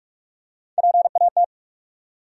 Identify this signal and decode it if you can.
Morse code, heard as a keyed tone, about 45 wpm, 700 Hz. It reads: PAT
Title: USA Common Names - Max 5 Letters - 45wpm